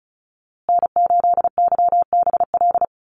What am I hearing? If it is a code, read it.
Morse code, heard as a keyed tone, about 34 wpm, 700 Hz. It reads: N8YBL
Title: USA Call Signs Repeated 3x 35wpm